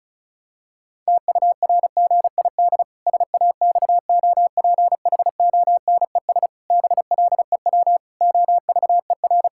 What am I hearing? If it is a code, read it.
Morse code, heard as a keyed tone, about 36 wpm, 700 Hz. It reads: TURGID SAXOPHONES BLEW OVER
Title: Pangram Challenge v2 - 35wpm